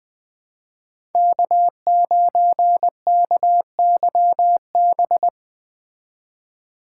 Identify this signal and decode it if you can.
Morse code, heard as a keyed tone, about 19 wpm, 700 Hz. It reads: K9KYB
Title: Speed Racing - USA 1x3 Call Signs 17wpm